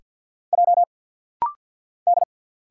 Morse code, heard as a keyed tone, about 48 wpm, 700 Hz. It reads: J D
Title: Single Letter-Number - Rapid-Fire 50wpm